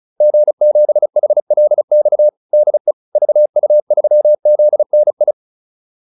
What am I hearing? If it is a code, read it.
Morse code, heard as a keyed tone, about 34 wpm, 600 Hz. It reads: G7HLX DE VU3ZNI